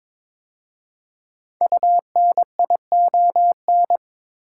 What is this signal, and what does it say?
Morse code, heard as a keyed tone, about 23 wpm, 700 Hz. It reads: UNION
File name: 5 Letter Words - First 3 Letters Given - 22wpm